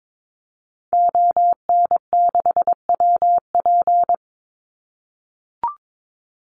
Morse code, heard as a keyed tone, about 22 wpm, 700 Hz. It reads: ON6WP
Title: Call Signs - 22wpm